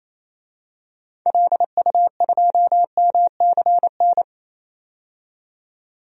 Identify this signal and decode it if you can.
Morse code, heard as a keyed tone, about 28 wpm, 700 Hz. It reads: LU2MCN
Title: DXCC-100 Call Signs - 6 Characters - 28wpm